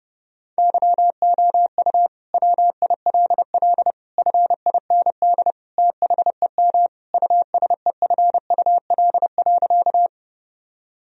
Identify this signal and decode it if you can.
Morse code, heard as a keyed tone, about 30 wpm, 700 Hz. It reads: YOU WILL FIND THEM USEFUL.
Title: Sentences from Top 1000 Words - Review Part 4 - 30wpm